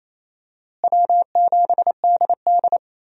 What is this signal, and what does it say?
Morse code, heard as a keyed tone, about 28 wpm, 700 Hz. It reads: W7DD
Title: USA Call Signs - 4 Characters - Repeated 3x - 28wpm